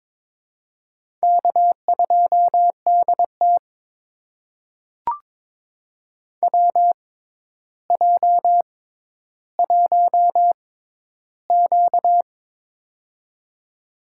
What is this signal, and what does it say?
Morse code, heard as a keyed tone, about 22 wpm, spaced as for 11 wpm, 700 Hz. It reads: K2DT WJ1Q
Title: USA Call Signs - 4 Characters - 6x Character Spacing - 22wpm